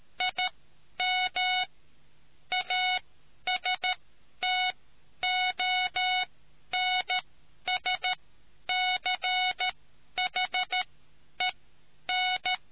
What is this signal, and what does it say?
Morse code, heard as a keyed tone, about 13 wpm, spaced as for 10 wpm, 740 Hz. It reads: IM ASTONSCHEN